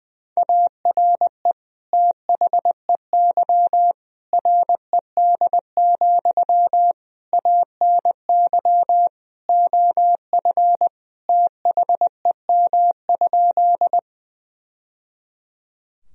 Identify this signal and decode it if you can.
Morse code, heard as a keyed tone, about 21 wpm, 700 Hz. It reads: ARE THEY RED, ANY OF THEM?
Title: Sentences from Top 1000 Words - Review Part 2 - 20wpm